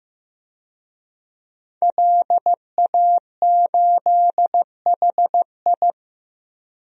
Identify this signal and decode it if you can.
Morse code, heard as a keyed tone, about 15 wpm, 700 Hz. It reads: LA8HI
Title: DXCC-100 Call Signs - 5 Characters - 15wpm